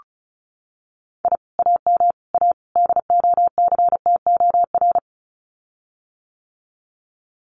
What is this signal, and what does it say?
Morse code, heard as a keyed tone, about 34 wpm, 700 Hz. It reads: I AM A DOCTOR
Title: Common QSO Phrases Encore 35wpm